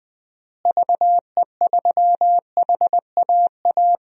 Morse code, heard as a keyed tone, about 20 wpm, 700 Hz. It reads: VE3HAA
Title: DXCC-100 Call Signs - 6 Characters - Repeated 2x - 20wpm